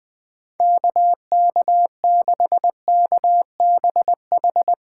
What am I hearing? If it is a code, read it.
Morse code, heard as a keyed tone, about 20 wpm, 700 Hz. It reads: KK6KBH
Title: Speed Racing - USA 2x3 Call Signs 17wpm